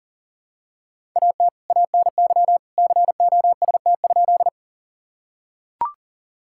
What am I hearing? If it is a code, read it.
Morse code, heard as a keyed tone, about 40 wpm, 700 Hz. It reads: AT ANY COST?
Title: Sentences from Top 1000 Words - Review Part 5 - 40wpm